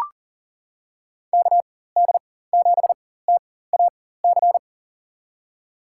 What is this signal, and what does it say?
Morse code, heard as a keyed tone, about 40 wpm, spaced as for 19 wpm, 700 Hz. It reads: KD7TAC
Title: USA Call Signs - 4x Character Spacing - 40wpm